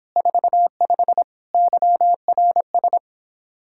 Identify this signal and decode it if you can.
Morse code, heard as a keyed tone, about 29 wpm, 700 Hz. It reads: 45 YRS